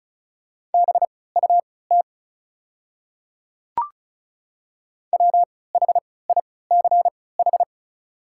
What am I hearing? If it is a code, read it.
Morse code, heard as a keyed tone, about 36 wpm, spaced as for 20 wpm, 690 Hz. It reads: BUT WHICH
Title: 3x Character Spacing - Top 100 Words - 35wpm